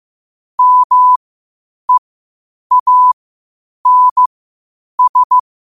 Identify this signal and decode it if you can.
Morse code, heard as a keyed tone, about 15 wpm, spaced as for 8 wpm, 1000 Hz. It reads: MEANS
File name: qotdfiles / 8wpm